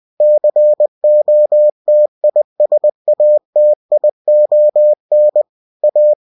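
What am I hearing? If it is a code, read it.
Morse code, heard as a keyed tone, about 20 wpm, 600 Hz. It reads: COTISATION A
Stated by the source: Bulletin sent in CW.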